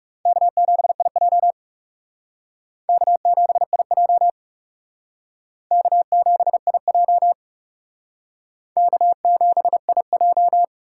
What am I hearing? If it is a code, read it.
Morse code, heard as a keyed tone, about 35 wpm, 700 Hz. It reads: K7IJ K7IJ K7IJ K7IJ
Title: Speed Racing - USA 1x2 Call Signs 30wpm